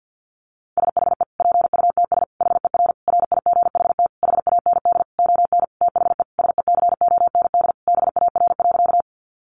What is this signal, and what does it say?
Morse code, heard as a keyed tone, about 50 wpm, 700 Hz. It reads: SHE PUTS HER RIGHT HAND ON THE SECOND BAR.
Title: Easy Phrases 50wpm